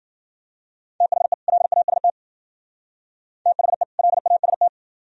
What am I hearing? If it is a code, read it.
Morse code, heard as a keyed tone, about 51 wpm, 700 Hz. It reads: THE LAST THE LAST